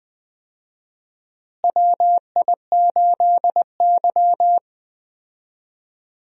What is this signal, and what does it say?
Morse code, heard as a keyed tone, about 20 wpm, 700 Hz. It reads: WI8Y
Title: USA Call Signs - 4 Characters - Repeated 3x - 20wpm